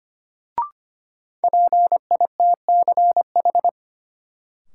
Morse code, heard as a keyed tone, about 25 wpm, 700 Hz. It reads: PITCH
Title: Sentences from Top 1300 Words - New Word Drill - 25wpm